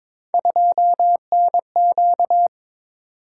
Morse code, heard as a keyed tone, about 22 wpm, 700 Hz. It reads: 2NQ
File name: practice / 22wpm / Three Letter-Number Combinations - 22wpm